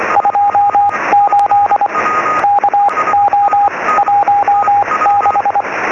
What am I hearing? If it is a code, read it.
Morse code, heard as a keyed tone, about 24 wpm, spaced as for 19 wpm, 820 Hz, with radio noise. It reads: 28 KO16
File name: sm7lcb / 0558 yl2aj 23cm